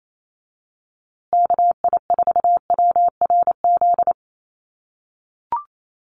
Morse code, heard as a keyed tone, about 28 wpm, 700 Hz. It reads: KI4WRZ
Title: USA Call Signs Repeated 3x 28wpm